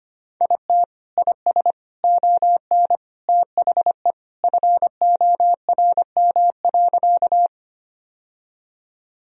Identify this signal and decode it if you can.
Morse code, heard as a keyed tone, about 25 wpm, 700 Hz. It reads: IT IS ON THE FORM.
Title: Sentences from Top 1000 Words - Review Part 1 - 25wpm